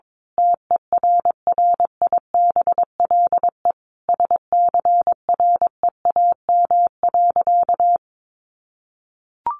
Morse code, heard as a keyed tone, about 22 wpm, 700 Hz. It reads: TERRIBLE SCREAM.
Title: A Sound Of Thunder 22wpm